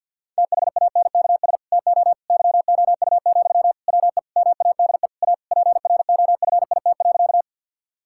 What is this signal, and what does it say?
Morse code, heard as a keyed tone, about 51 wpm, 700 Hz. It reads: THANKS TO YOU, WE MADE A PROFIT.
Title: Sentences from Top 1000 Words - Review Part 5 - 50wpm